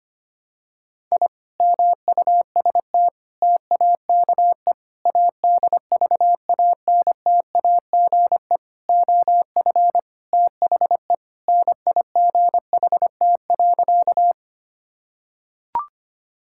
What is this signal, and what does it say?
Morse code, heard as a keyed tone, about 25 wpm, 700 Hz. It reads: I MUST TAKE ADVANTAGE OF THE NIGHT.
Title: Sentences from Top 1000 Words - Review Part 6 - 25wpm